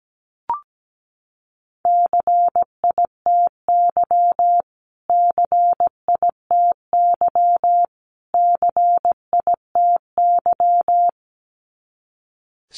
Morse code, heard as a keyed tone, about 17 wpm, 700 Hz. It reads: CITY CITY CITY
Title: Top 300 Words Repeated 17wpm